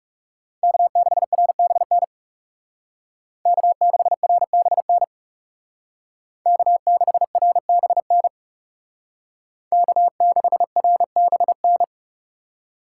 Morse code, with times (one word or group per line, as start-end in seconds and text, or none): 0.6-2.0 K6RBN
3.5-5.0 K6RBN
6.5-8.3 K6RBN
9.7-11.8 K6RBN